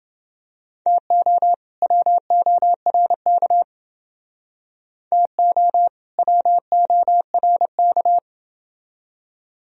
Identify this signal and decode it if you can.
Morse code, heard as a keyed tone, about 28 wpm, 700 Hz. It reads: TO WORK TO WORK